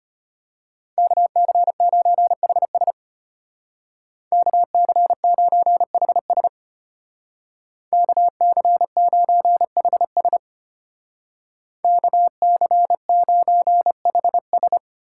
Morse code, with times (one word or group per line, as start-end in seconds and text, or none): 1.0-2.9 KC9HS
4.3-6.5 KC9HS
7.9-10.4 KC9HS
11.8-14.8 KC9HS